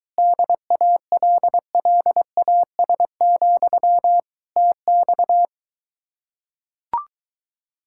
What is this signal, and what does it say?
Morse code, heard as a keyed tone, about 23 wpm, 700 Hz. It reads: DALLAS, TX